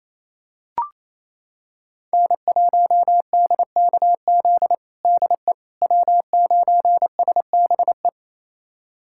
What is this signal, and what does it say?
Morse code, heard as a keyed tone, about 28 wpm, 700 Hz. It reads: N1DKZ DE W9SBE
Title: USA Callsign DE Callsign - 5 Characters - 28wpm